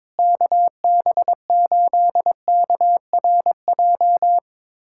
Morse code, heard as a keyed tone, about 22 wpm, 700 Hz. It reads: KB8KRJ
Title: USA Callsign DE Callsign 22wpm